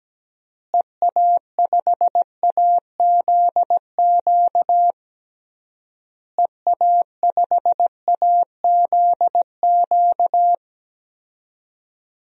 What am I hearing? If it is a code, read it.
Morse code, heard as a keyed tone, about 17 wpm, 700 Hz. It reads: EA5AZQ EA5AZQ
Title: DXCC-100 Call Signs - 6 Characters - Repeated 3x - 17wpm